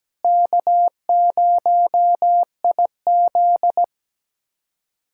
Morse code, heard as a keyed tone, about 17 wpm, 700 Hz. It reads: K0IZ
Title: K1USN SST QSO Practice - 17wpm